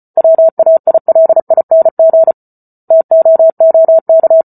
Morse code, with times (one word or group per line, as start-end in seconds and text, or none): 0.2-2.3 WAILING
2.9-4.4 TOOK